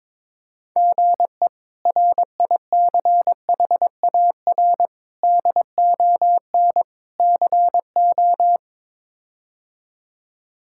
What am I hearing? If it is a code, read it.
Morse code, heard as a keyed tone, about 22 wpm, 700 Hz. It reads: GE RICHAR DON CO
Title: K1USN SST QSO Practice - 22wpm